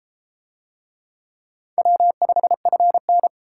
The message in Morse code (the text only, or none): W5FN